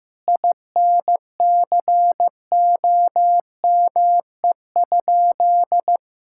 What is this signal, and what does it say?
Morse code, heard as a keyed tone, about 15 wpm, 700 Hz. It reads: INCOME?
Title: Sentences from Top 1000 Words - Review Part 5 - 15wpm